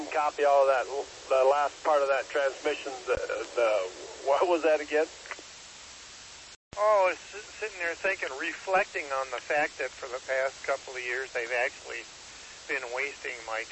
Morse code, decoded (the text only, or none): none